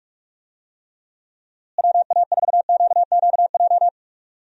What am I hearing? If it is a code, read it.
Morse code, heard as a keyed tone, about 41 wpm, 700 Hz. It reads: WA4QQJ
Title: USA Call Signs - 6 Characters - Repeated 2x - 45wpm